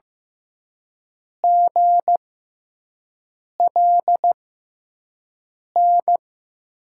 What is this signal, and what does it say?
Morse code, heard as a keyed tone, about 15 wpm, spaced as for 5 wpm, 700 Hz. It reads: GLN
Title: USA Call Signs - 5 Characters - 6x Character Spacing - 15wpm